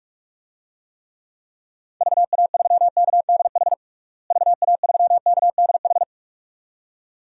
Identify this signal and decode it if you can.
Morse code, heard as a keyed tone, about 41 wpm, 700 Hz. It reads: VA3KDH VA3KDH
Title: DXCC-100 Call Signs - 6 Characters - Repeated 2x - 45wpm